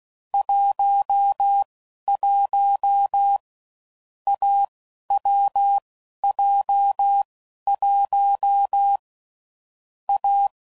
Morse code, heard as a keyed tone, about 16 wpm, spaced as for 11 wpm, 800 Hz. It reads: 11 AWJ1 A
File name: Combinations / Difficult Combos / AWJ1